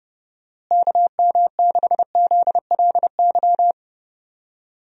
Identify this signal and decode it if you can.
Morse code, heard as a keyed tone, about 30 wpm, 700 Hz. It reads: KM6ZLY
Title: USA Call Signs - 6 Characters - Repeated 2x - 30wpm